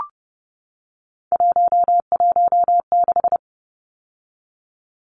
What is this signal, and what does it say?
Morse code, heard as a keyed tone, about 30 wpm, 700 Hz. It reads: 116
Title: Numbers - 3-Digit 30wpm